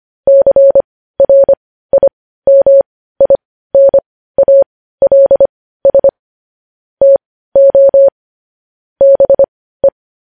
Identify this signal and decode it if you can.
Morse code, heard as a keyed tone, about 25 wpm, spaced as for 15 wpm, 550 Hz. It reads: CRIMINALS TO BE
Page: news headlines in morse code at 15 wpm